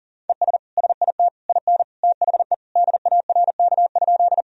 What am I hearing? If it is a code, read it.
Morse code, heard as a keyed tone, about 40 wpm, 700 Hz. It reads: ES SIT IN THE DARK?
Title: Sentences from Top 500 Words and Abbreviations 40wpm